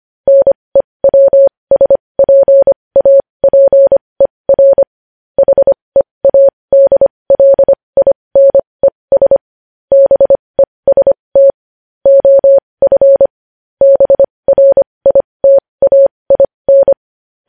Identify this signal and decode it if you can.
Morse code, heard as a keyed tone, about 25 wpm, spaced as for 20 wpm, 550 Hz. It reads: NEWSPAPER HEADLINES BEST OF BRITAIN
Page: News headlines in Morse code at 20 WPM